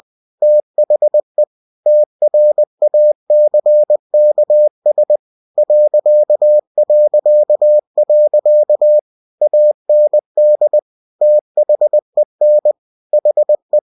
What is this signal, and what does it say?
Morse code, heard as a keyed tone, about 20 wpm, 600 Hz. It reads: THE TRACKS ... AND THEN HE
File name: cw-books / winnie-the-pooh / winnie-20wpm-page-0025